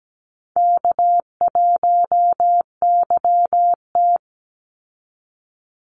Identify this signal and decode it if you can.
Morse code, heard as a keyed tone, about 17 wpm, 700 Hz. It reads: K1YT
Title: Call Signs Repeated 3x - 17wpm